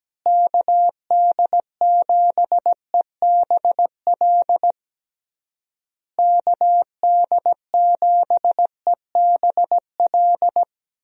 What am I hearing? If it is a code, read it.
Morse code, heard as a keyed tone, about 17 wpm, 700 Hz. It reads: KD7EBL KD7EBL
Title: USA Call Signs - 6 Characters - Repeated 3x - 17wpm